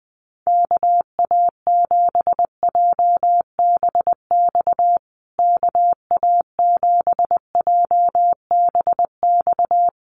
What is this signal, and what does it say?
Morse code, heard as a keyed tone, about 20 wpm, 700 Hz. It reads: KA7JBX KA7JBX
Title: USA Call Signs Repeated 3x 20wpm